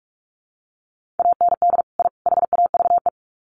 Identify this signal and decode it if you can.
Morse code, heard as a keyed tone, about 44 wpm, 700 Hz. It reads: AND I HAVE
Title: Sets of 3 Words 45wpm